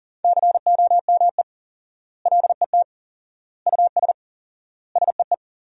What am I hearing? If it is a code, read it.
Morse code, heard as a keyed tone, about 40 wpm, 700 Hz. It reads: COME LET US SEE
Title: 4x Word-Spacing - Sentences from Top 300 Words - 40wpm